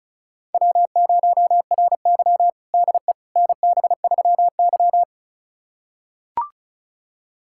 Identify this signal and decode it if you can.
Morse code, heard as a keyed tone, about 36 wpm, 700 Hz. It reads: W0RY DE NB3Y
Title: USA Callsign DE Callsign - 4 Characters - 35wpm